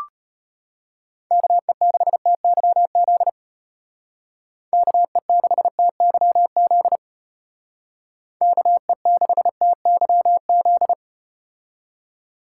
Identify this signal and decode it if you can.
Morse code, heard as a keyed tone, about 34 wpm, 700 Hz. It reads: KE6TYZ KE6TYZ KE6TYZ